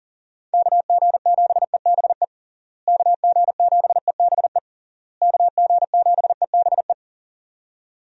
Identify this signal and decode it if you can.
Morse code, heard as a keyed tone, about 40 wpm, 700 Hz. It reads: KG7EBE KG7EBE KG7EBE
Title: USA Call Signs - 6 Characters - Repeated 3x - 40wpm